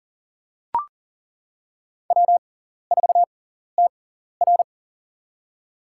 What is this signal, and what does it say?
Morse code, heard as a keyed tone, about 40 wpm, spaced as for 14 wpm, 700 Hz. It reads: W4TR
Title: USA Call Signs - 4 Characters - 6x Character Spacing - 40wpm